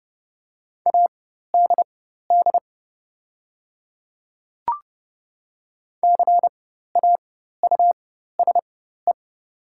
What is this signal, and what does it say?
Morse code, heard as a keyed tone, about 31 wpm, spaced as for 14 wpm, 700 Hz. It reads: ADD CAUSE
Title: 4x Character Spacing - Top 300 Words - 30wpm